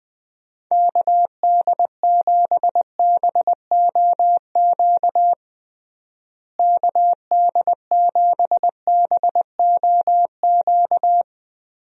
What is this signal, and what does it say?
Morse code, heard as a keyed tone, about 21 wpm, 700 Hz. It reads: KD7BOQ KD7BOQ
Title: USA Call Signs - 6 Characters - Repeated 3x - 20wpm